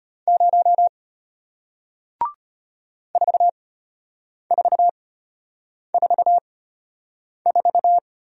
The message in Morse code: 0 4444

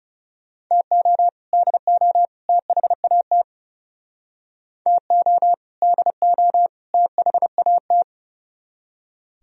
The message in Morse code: TO DO THAT TO DO THAT

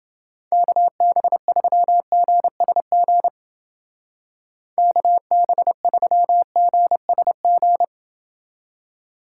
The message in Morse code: KB3GSG KB3GSG

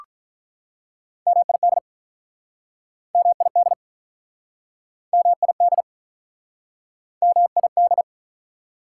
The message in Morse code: MID MID MID MID

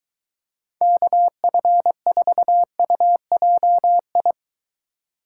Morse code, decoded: KF4UJI

